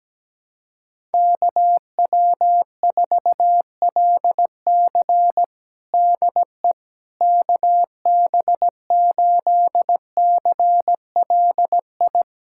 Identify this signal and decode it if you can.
Morse code, heard as a keyed tone, about 17 wpm, 700 Hz. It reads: KW4LC DE KB8CLI
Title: USA Callsign DE Callsign 17wpm